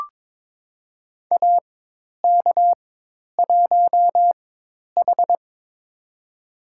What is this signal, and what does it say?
Morse code, heard as a keyed tone, about 22 wpm, spaced as for 10 wpm, 700 Hz. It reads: AK1H
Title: USA Call Signs - 4 Characters - 4x Character Spacing - 22wpm